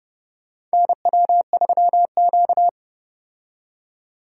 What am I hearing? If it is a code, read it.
Morse code, heard as a keyed tone, about 30 wpm, 700 Hz. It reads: NW3Q
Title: USA Call Signs - 4 Characters - 30wpm